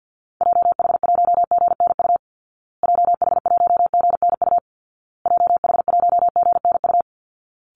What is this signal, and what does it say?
Morse code, heard as a keyed tone, about 50 wpm, 700 Hz. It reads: JH1GNU JH1GNU JH1GNU
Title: Call Signs Repeated 2x - 50wpm